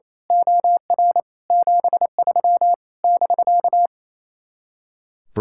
Morse code, heard as a keyed tone, about 27 wpm, 700 Hz. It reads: OR 73 <BK>
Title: Speed-Racing Common QSO Phrases - 28wpm